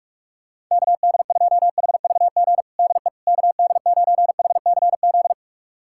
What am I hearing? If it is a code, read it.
Morse code, heard as a keyed tone, about 44 wpm, 700 Hz. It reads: KD2HVG DE KD9HCZ
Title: USA Callsign DE Callsign - 6 Characters - 45wpm